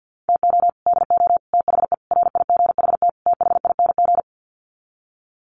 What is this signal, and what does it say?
Morse code, heard as a keyed tone, about 50 wpm, 700 Hz. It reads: TO DO THE RIGHT THING
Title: Sets of 5 Words 50wpm